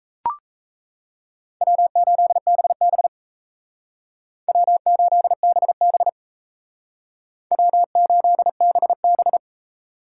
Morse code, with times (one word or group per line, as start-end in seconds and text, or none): 1.6-3.1 W8BB
4.5-6.1 W8BB
7.5-9.4 W8BB